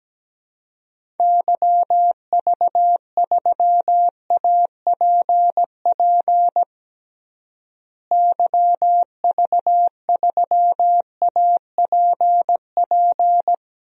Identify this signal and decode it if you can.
Morse code, heard as a keyed tone, about 17 wpm, 700 Hz. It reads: YV3APP YV3APP
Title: DXCC-100 Call Signs - 6 Characters - Repeated 2x - 17wpm